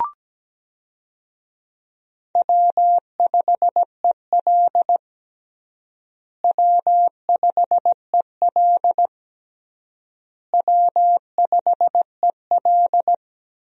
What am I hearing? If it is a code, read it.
Morse code, heard as a keyed tone, about 17 wpm, 700 Hz. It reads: W5EL W5EL W5EL
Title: USA Call Signs - 4 Characters - Repeated 3x - 17wpm